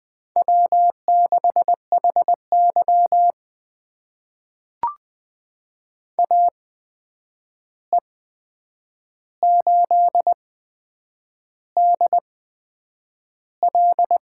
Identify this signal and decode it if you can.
Morse code, heard as a keyed tone, about 20 wpm, spaced as for 8 wpm, 700 Hz. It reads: W6HY AE8DL
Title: USA Call Signs - 8x Character Spacing - 20wpm